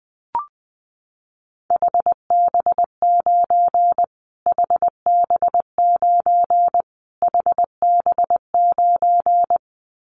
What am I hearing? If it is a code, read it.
Morse code, heard as a keyed tone, about 19 wpm, 700 Hz. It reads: HB9 HB9 HB9
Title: Callsign Prefixes for Top 25 DXCC - Repeated 3x - 20wpm